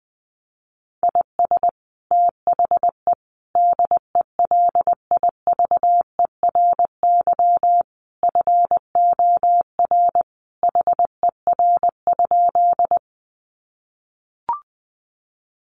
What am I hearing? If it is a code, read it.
Morse code, heard as a keyed tone, about 20 wpm, 700 Hz. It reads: IS THE DELIVERY FOR HER?
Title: Sentences from Top 1500 Words 20wpm